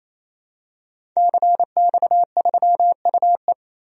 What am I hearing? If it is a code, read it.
Morse code, heard as a keyed tone, about 27 wpm, 700 Hz. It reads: CX3UE
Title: DXCC-100 Call Signs - 5 Characters - Repeated 3x - 28wpm